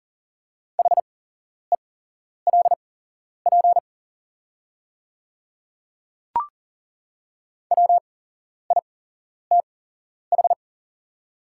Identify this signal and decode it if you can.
Morse code, heard as a keyed tone, about 39 wpm, spaced as for 11 wpm, 690 Hz. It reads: HELP WITH
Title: 8x Character Spacing - Top 200 Words - 40wpm